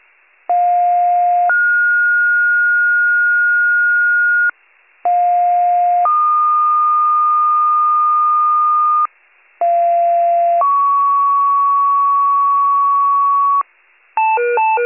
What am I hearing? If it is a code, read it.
Two-tone page, heard as a one-second tone, then a three-second tone, about 710 then 1430 Hz, about 710 then 1220 Hz, about 690 then 1120 Hz. It is heard from 0.5 to 13.6 s.